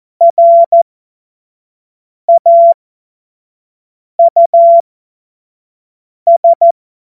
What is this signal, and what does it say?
Morse code, heard as a keyed tone, about 14 wpm, spaced as for 5 wpm, 680 Hz. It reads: RAUS
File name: Morsekurs / Witze / Witz 02. November 2021, 25 F70